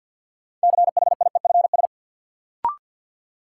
Morse code, heard as a keyed tone, about 53 wpm, 700 Hz. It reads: CHIEFS
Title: Sets of 5 Words Encore - New Word Drill - 50wpm